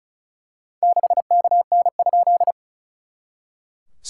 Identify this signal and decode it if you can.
Morse code, heard as a keyed tone, about 36 wpm, 690 Hz. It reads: 6KN?
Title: SOTA QSO Practice 35wpm